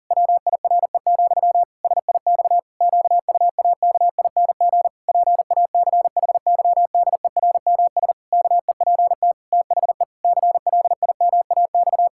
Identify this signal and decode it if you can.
Morse code, heard as a keyed tone, about 40 wpm, 700 Hz. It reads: WIRE, SIX QUAKING PACHYDERMS KEPT THE CLIMAX
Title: Pangram Challenge v2 - 40wpm